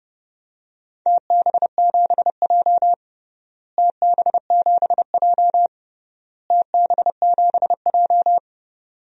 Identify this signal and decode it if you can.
Morse code, heard as a keyed tone, about 32 wpm, 700 Hz. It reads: TB7J TB7J TB7J
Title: DXCC-100 Call Signs - 4 Characters - Repeated 3x - 30wpm